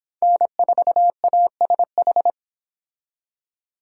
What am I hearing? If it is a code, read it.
Morse code, heard as a keyed tone, about 26 wpm, 700 Hz. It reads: N4ASH